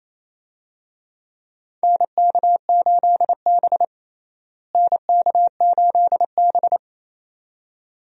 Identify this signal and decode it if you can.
Morse code, heard as a keyed tone, about 28 wpm, 700 Hz. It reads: NK8B NK8B
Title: USA Call Signs - 4 Characters - Repeated 2x - 28wpm